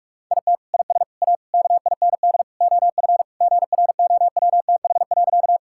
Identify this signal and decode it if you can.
Morse code, heard as a keyed tone, about 46 wpm, 700 Hz. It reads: IT IS A KIND OF GROWTH.
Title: Sentences from Top 1000 Words - Review Part 5 - 45wpm